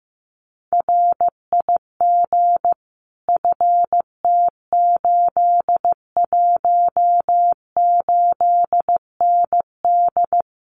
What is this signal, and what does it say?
Morse code, heard as a keyed tone, about 14 wpm, 700 Hz. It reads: RIG FT818ND
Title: Common QSO Phrases Encore 15wpm